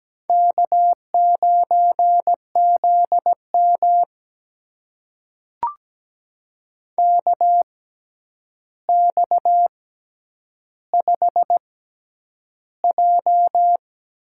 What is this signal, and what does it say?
Morse code, heard as a keyed tone, about 17 wpm, spaced as for 9 wpm, 700 Hz. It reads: K9ZM KX5J
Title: USA Call Signs - 4 Characters - 6x Character Spacing - 17wpm